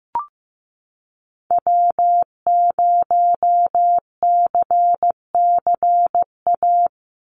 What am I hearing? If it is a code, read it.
Morse code, heard as a keyed tone, about 15 wpm, 700 Hz. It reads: W0CCA
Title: NA SOTA Activator Callsigns 15wpm